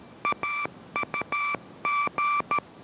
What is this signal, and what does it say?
Morse code, heard as a keyed tone, about 13 wpm, 1170 Hz, with noise. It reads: AUG